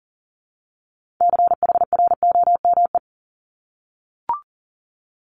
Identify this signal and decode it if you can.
Morse code, heard as a keyed tone, about 37 wpm, 700 Hz. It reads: CHROME
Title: Multi-letter Phonemes Repeated 3x 40wpm